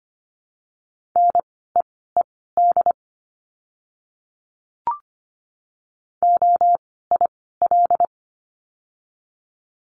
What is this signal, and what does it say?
Morse code, heard as a keyed tone, about 25 wpm, spaced as for 16 wpm, 700 Hz. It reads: NEED OIL
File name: course / 25:10wpm / 094 Lesson 16f - Letter c - Words 25wpm